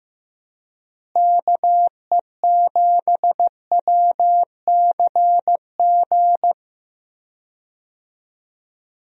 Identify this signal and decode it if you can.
Morse code, heard as a keyed tone, about 15 wpm, 700 Hz. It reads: KE7WCG
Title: USA Call Signs - 6 Characters - Repeated 3x - 15wpm